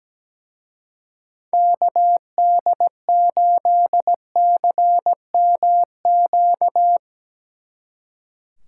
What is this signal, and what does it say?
Morse code, heard as a keyed tone, about 18 wpm, 700 Hz. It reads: KD8CMQ
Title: Speed Racing - USA 2x3 Call Signs 17wpm